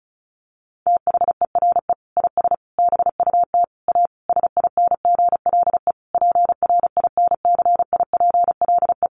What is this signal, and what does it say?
Morse code, heard as a keyed tone, about 34 wpm, 700 Hz. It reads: THERE IS BUT A SINGLE PRINCIPLE
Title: Sentences from Top 1400 Words 35wpm